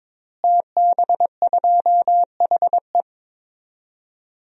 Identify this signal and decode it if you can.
Morse code, heard as a keyed tone, about 22 wpm, 700 Hz. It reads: TB2HE
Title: DXCC-100 Call Signs - 5 Characters - Repeated 3x - 22wpm